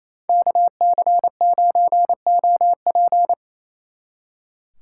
Morse code, heard as a keyed tone, about 33 wpm, 700 Hz. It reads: KC9OP
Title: Speed Racing - USA 2x2 Call Signs 28wpm